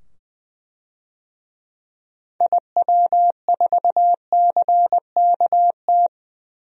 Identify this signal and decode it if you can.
Morse code, heard as a keyed tone, about 21 wpm, 700 Hz. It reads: IW4CKT
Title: DXCC-100 Call Signs - 6 Characters - 20wpm